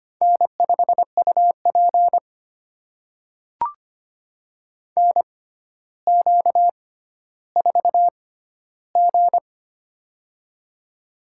Morse code, heard as a keyed tone, about 25 wpm, spaced as for 13 wpm, 700 Hz. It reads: N5UP NQ4G
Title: USA Call Signs - 4 Characters - 6x Character Spacing - 25wpm